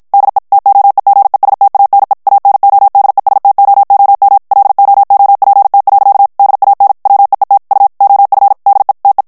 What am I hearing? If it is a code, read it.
Morse code, heard as a keyed tone, about 53 wpm, 800 Hz. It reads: DE TOEGESTANE ANODESTOOM LOOPT. DAN WEET U OF DE TE
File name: HOLLANDAIS texte clair 2 vitesse 50 mots minute